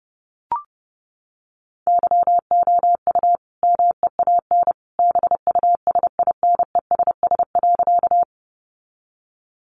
Morse code, heard as a keyed tone, about 30 wpm, 700 Hz. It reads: YOU MEAN BUSINESS.
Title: Sentences from Top 600 Words 30wpm